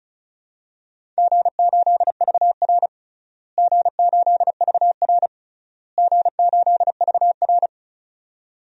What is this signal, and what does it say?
Morse code, heard as a keyed tone, about 36 wpm, 700 Hz. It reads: G8VR G8VR G8VR
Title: DXCC-100 Call Signs - 4 Characters - Repeated 3x - 35wpm